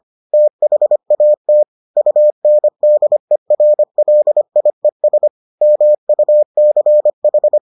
Morse code, heard as a keyed tone, about 25 wpm, 600 Hz. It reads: THAT UNDERLIES MUCH